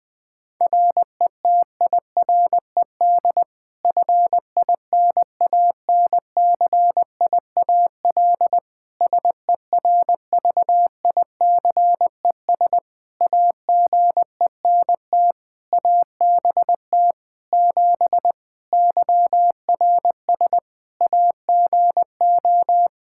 Morse code, heard as a keyed tone, about 20 wpm, 700 Hz. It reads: RETIRED FINANCIAL SERVICES AGENT ABT 7 YRS AGO
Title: K1USN SST QSO Practice 20wpm